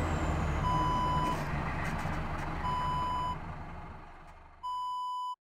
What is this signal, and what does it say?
There is no code.